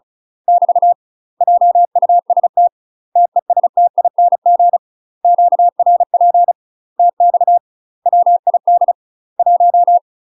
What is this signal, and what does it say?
Morse code, heard as a keyed tone, about 36 wpm, 700 Hz. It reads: <BT> JUST TESTING QRP TX WID 1
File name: UK MASTER QSOs TRAINING/35WPM